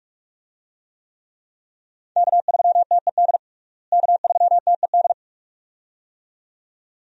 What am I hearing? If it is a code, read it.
Morse code, heard as a keyed tone, about 42 wpm, 700 Hz. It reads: K3TED K3TED
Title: USA Call Signs - 5 Characters - Repeated 2x - 45wpm